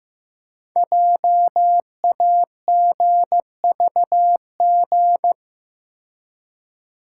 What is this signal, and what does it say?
Morse code, heard as a keyed tone, about 15 wpm, 700 Hz. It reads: JAGVG
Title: Code Groups - 5 Letters - 15wpm